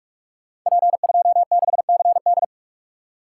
Morse code, heard as a keyed tone, about 44 wpm, 700 Hz. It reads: P26CD
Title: Code Groups - 5 Letter:Numbers - 45wpm